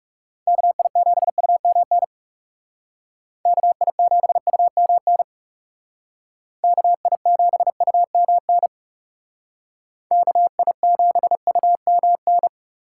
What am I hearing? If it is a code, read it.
Morse code, heard as a keyed tone, about 37 wpm, 700 Hz. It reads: KI7UMN KI7UMN KI7UMN KI7UMN
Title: Speed Racing - USA 2x3 Call Signs 30wpm